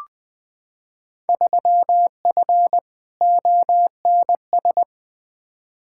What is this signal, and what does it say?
Morse code, heard as a keyed tone, about 20 wpm, 700 Hz. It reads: 3F ONS
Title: Field Day Exchanges - 20wpm